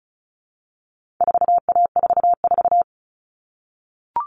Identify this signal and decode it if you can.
Morse code, heard as a keyed tone, about 34 wpm, 700 Hz. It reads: 4A44
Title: All Cut Numbers - 4-Digit 35wpm